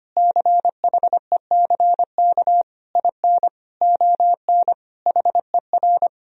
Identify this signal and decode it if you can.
Morse code, heard as a keyed tone, about 25 wpm, 700 Hz. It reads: CHECK IN ON HER
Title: Sentences from Top 1000 Words - Review Part 3 - 25wpm